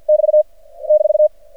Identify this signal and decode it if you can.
Morse code, heard as a keyed tone, about 53 wpm, 610 Hz. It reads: <BT> -
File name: archived / CRSnet